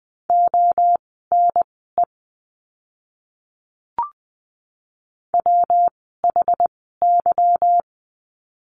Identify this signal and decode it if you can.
Morse code, heard as a keyed tone, about 20 wpm, spaced as for 14 wpm, 700 Hz. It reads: ONE WHY
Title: 192 Lesson 31f - Letter v - Words 20wpm